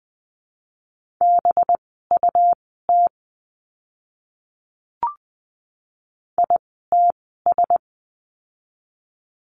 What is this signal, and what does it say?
Morse code, heard as a keyed tone, about 20 wpm, spaced as for 14 wpm, 700 Hz. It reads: BUT ITS
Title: 206 Lesson 33f - Letter j - Words 20wpm